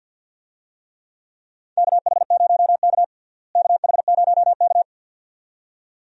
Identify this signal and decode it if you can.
Morse code, heard as a keyed tone, about 50 wpm, 700 Hz. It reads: KH0K KH0K
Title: USA Call Signs - 4 Characters - Repeated 2x - 50wpm